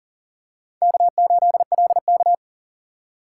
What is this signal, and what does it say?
Morse code, heard as a keyed tone, about 40 wpm, 700 Hz. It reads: K8LK
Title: USA Call Signs - 4 Characters - 2x Character Spacing - 40wpm